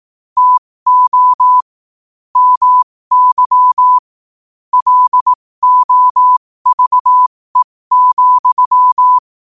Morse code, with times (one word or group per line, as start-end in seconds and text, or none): 0.4-1.6 TO
2.4-4.0 MY
4.7-9.2 LOVE,